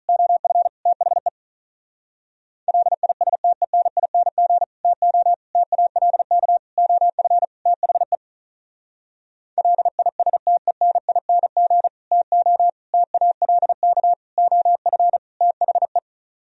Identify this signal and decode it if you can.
Morse code, heard as a keyed tone, about 45 wpm, 700 Hz. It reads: OF THE LISTENING TO TALK OF THE LISTENING TO TALK OF THE